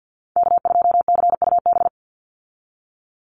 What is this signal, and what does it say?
Morse code, heard as a keyed tone, about 49 wpm, 700 Hz. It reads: K2CUB
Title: Call Signs Repeated 3x - 50wpm